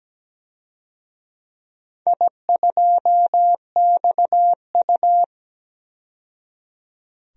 Morse code, heard as a keyed tone, about 17 wpm, 700 Hz. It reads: I2XU
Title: DXCC-100 Call Signs - 4 Characters - 17wpm